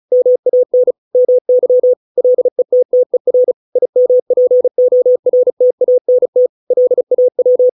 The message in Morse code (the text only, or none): MAN MY LETTER IMPORTANT LAW